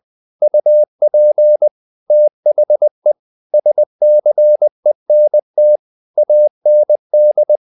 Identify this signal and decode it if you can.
Morse code, heard as a keyed tone, about 20 wpm, 600 Hz. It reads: UP THE SCENT AND
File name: the old man and the sea part 2 20wpm